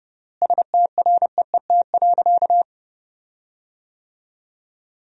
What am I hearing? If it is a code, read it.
Morse code, heard as a keyed tone, about 32 wpm, 700 Hz. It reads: STREET.
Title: 3x Word-Spacing - Sentences from Top 500 Words - 30wpm